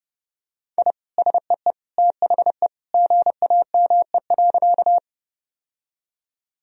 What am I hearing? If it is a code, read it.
Morse code, heard as a keyed tone, about 32 wpm, 700 Hz. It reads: I SEE THE GAME.
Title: Sentences from Top 1000 Words - Review Part 3 - 30wpm